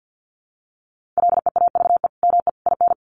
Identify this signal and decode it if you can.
Morse code, heard as a keyed tone, about 50 wpm, 700 Hz. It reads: LEAVE ME IN
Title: Sentences from Top 1500 Words 50wpm